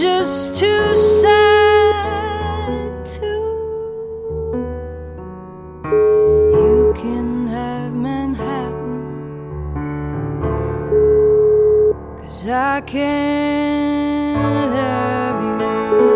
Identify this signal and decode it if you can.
Ringback tone (Europe, 425 Hz), with noise.